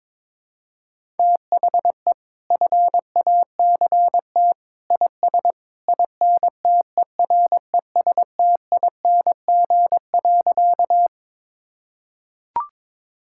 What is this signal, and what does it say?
Morse code, heard as a keyed tone, about 22 wpm, 700 Hz. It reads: THE FACT IS INTERESTING.
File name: practice / 22wpm / Sentences with 2-Word Phrases - 22wpm